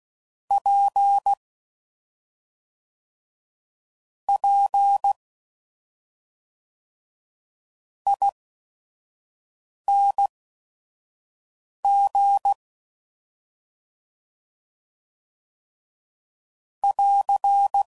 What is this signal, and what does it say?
Morse code, heard as a keyed tone, about 16 wpm, spaced as for 4 wpm, 800 Hz. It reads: PPING <AR>